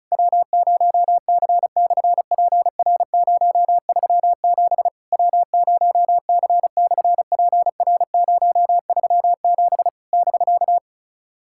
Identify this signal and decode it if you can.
Morse code, heard as a keyed tone, about 36 wpm, 690 Hz. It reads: W0C/PR037 W0C/PR037 <BK>
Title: SOTA QSO Practice 35wpm